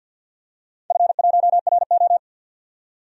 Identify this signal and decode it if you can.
Morse code, heard as a keyed tone, about 52 wpm, 700 Hz. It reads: F1RO
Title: DXCC-100 Call Signs - 4 Characters - 50wpm